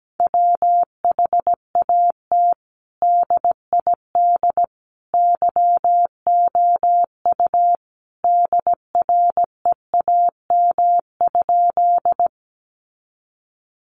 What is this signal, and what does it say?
Morse code, heard as a keyed tone, about 17 wpm, 700 Hz. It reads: WHAT DID YOU DREAM?
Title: Sentences from Top 1500 Words 17wpm